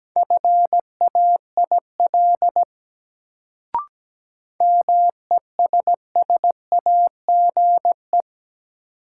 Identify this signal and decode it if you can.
Morse code, heard as a keyed tone, about 17 wpm, 700 Hz. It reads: FAIL MESSAGE
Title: Sentences from Top 1100 Words - New Word Drill - 17wpm